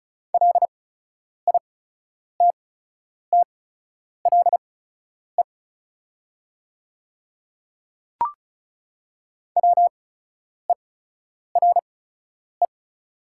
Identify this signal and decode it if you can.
Morse code, heard as a keyed tone, about 35 wpm, spaced as for 10 wpm, 690 Hz. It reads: LITTLE WERE
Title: 8x Character Spacing - Top 300 Words - 35wpm